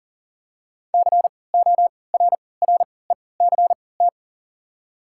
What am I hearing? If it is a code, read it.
Morse code, heard as a keyed tone, about 40 wpm, spaced as for 23 wpm, 690 Hz. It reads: CORRECT
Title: 3x Character Spacing - Top 500 Words - 40wpm